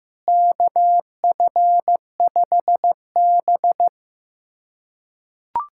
Morse code, heard as a keyed tone, about 15 wpm, 700 Hz. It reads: KF5B
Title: USA Callsign DE Callsign - 4 Characters - 15wpm